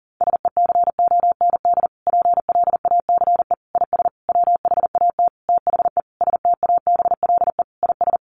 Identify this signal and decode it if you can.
Morse code, heard as a keyed tone, about 40 wpm, 700 Hz. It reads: SECOND PLACE IS WHAT THE STABLE IS
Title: Sentences from Top 1500 Words 40wpm